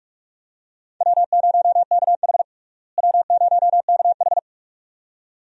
Morse code, heard as a keyed tone, about 46 wpm, 700 Hz. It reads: W0KH W0KH
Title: USA Call Signs - 4 Characters - Repeated 2x - 45wpm